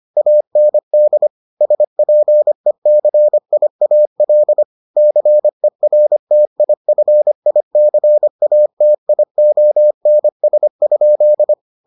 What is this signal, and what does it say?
Morse code, heard as a keyed tone, about 25 wpm, 600 Hz. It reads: AND SPECIAL CERTIFICATIONS?